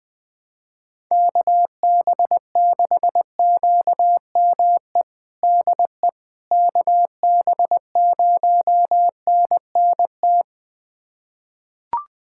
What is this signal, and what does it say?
Morse code, heard as a keyed tone, about 21 wpm, 700 Hz. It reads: KB6QME DE KB0NNT
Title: USA Callsign DE Callsign - 6 Characters - 20wpm